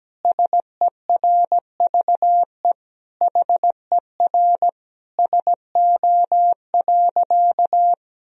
Morse code, heard as a keyed tone, about 17 wpm, 700 Hz. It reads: SERVE HER SO.
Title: Sentences from Top 1000 Words - Review Part 2 - 17wpm